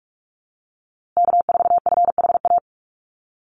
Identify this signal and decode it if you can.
Morse code, heard as a keyed tone, about 44 wpm, 700 Hz. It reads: K4FHA
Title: USA Call Signs Repeated 2x 45wpm